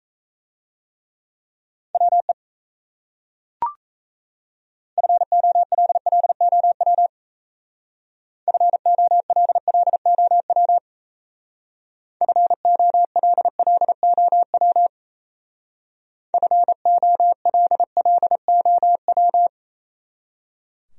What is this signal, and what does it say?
Morse code, heard as a keyed tone, about 37 wpm, 700 Hz. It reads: WE FOLLOW FOLLOW FOLLOW FOLLOW